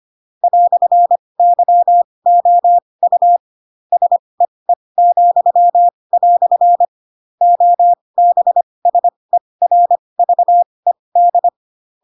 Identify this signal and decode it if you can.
Morse code, heard as a keyed tone, about 25 wpm, spaced as for 20 wpm, 700 Hz. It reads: "YOU SEE," OBSERVED
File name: Pinocchio / Pinocchio ch7 15 25 ik1hge